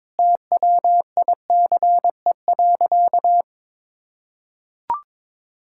Morse code, heard as a keyed tone, about 22 wpm, 700 Hz. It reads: TWICE.
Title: Sentences from Top 1600 Words 22wpm